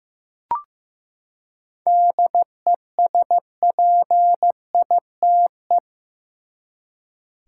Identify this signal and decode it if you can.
Morse code, heard as a keyed tone, about 16 wpm, 700 Hz. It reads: DESPITE
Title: Seven Letter Words - 15wpm